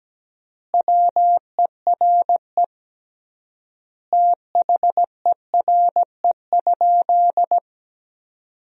Morse code, heard as a keyed tone, about 17 wpm, 700 Hz. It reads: WERE THERE?
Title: 3x Word-Spacing - Sentences from Top 500 Words - 17wpm